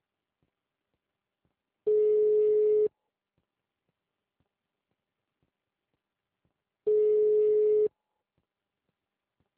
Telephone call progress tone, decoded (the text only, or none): ringback tone